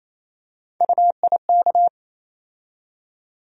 Morse code, heard as a keyed tone, about 28 wpm, 700 Hz. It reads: UIK
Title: USA Call Sign Postfixes - 3 Letters 28wpm